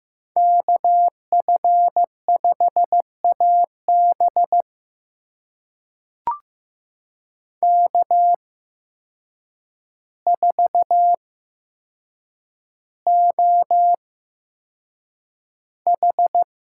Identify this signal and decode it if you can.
Morse code, heard as a keyed tone, about 15 wpm, spaced as for 7 wpm, 700 Hz. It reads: KF5AB K4OH
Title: USA Call Signs - 8x Character Spacing - 15wpm